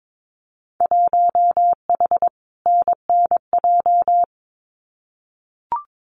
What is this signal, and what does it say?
Morse code, heard as a keyed tone, about 22 wpm, 700 Hz. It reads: 1H NNJ
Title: Winter Field Day Exchanges 22wpm